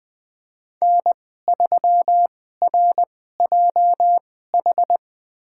Morse code, heard as a keyed tone, about 20 wpm, spaced as for 14 wpm, 700 Hz. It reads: N3RJH
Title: USA Call Signs - 5 Characters - 2x Character Spacing - 20wpm